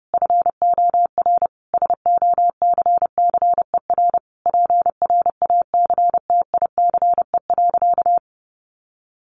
Morse code, heard as a keyed tone, about 30 wpm, 700 Hz. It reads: FOR SOCCER PRACTICE.
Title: Easy Phrases 30wpm